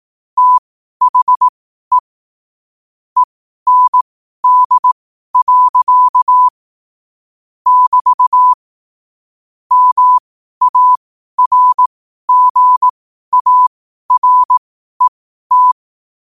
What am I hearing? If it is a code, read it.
Morse code, heard as a keyed tone, about 18 wpm, spaced as for 12 wpm, 1000 Hz. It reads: THE END. <BT> MARGARET
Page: New quotes every day in morse code at 12 Words per minute.